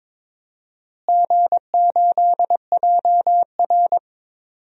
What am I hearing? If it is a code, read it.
Morse code, heard as a keyed tone, about 22 wpm, 700 Hz. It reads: G8JR
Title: DXCC-100 Call Signs - 4 Characters - Repeated 2x - 22wpm